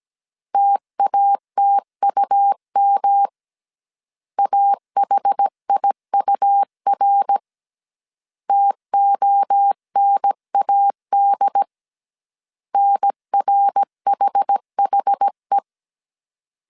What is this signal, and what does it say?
Morse code, heard as a keyed tone, about 17 wpm, 800 Hz. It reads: TATUM AHIUR TONAD NRHHE
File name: CW / DET / 1 3 1500C S C 16wpm 2148